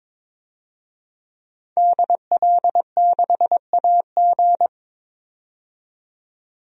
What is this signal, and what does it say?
Morse code, heard as a keyed tone, about 22 wpm, 700 Hz. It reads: DL6AG
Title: DXCC-100 Call Signs - 5 Characters - 22wpm